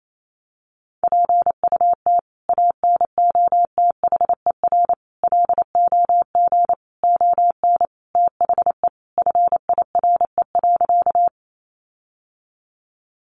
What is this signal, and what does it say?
Morse code, heard as a keyed tone, about 28 wpm, 700 Hz. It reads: PUT ANOTHER LOG ON THE FIRE.
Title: Sentences from Top 1400 Words 28wpm